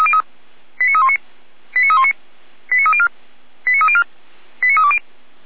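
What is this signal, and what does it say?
Five-tone selective call (ZVEI), heard as quick runs of five tones, tones of about 70 ms. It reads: 89319 89319 89304 89304 89320
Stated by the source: ZVEI five-tone selective call received by SDR.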